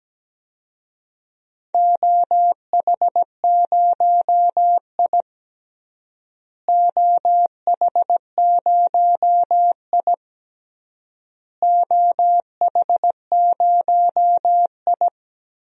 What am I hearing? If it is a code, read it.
Morse code, heard as a keyed tone, about 17 wpm, 700 Hz. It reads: OH0I OH0I OH0I
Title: DXCC-100 Call Signs - 4 Characters - Repeated 3x - 17wpm